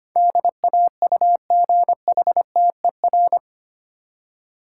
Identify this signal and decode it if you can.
Morse code, heard as a keyed tone, about 25 wpm, 700 Hz. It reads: DAUGHTER
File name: Eight Letter Words - 25wpm